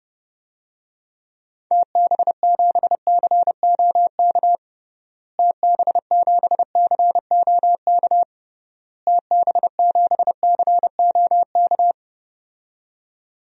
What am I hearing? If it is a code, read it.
Morse code, heard as a keyed tone, about 32 wpm, 700 Hz. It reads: TB7COK TB7COK TB7COK
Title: DXCC-100 Call Signs - 6 Characters - Repeated 3x - 30wpm